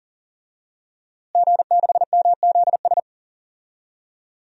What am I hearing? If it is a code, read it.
Morse code, heard as a keyed tone, about 40 wpm, 700 Hz. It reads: G6MZS
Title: DXCC-100 Call Signs - 5 Characters - Repeated 2x - 40wpm